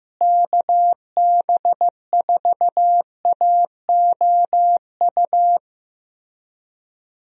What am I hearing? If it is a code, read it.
Morse code, heard as a keyed tone, about 15 wpm, 700 Hz. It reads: KB4AOU
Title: USA Call Signs - 6 Characters - Repeated 3x - 15wpm